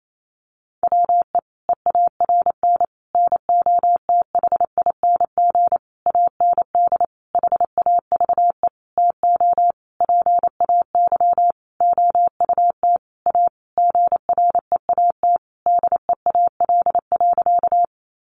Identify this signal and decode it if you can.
Morse code, heard as a keyed tone, about 28 wpm, 700 Hz. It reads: WE EARN NOTHING AND HAVE TO PAY OUT A GREAT DEAL.
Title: Sentences from Top 1500 Words 28wpm